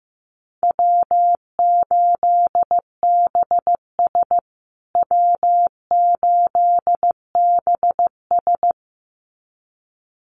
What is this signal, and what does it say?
Morse code, heard as a keyed tone, about 15 wpm, 700 Hz. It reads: W8BS W8BS
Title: USA Call Signs Repeated 2x 15wpm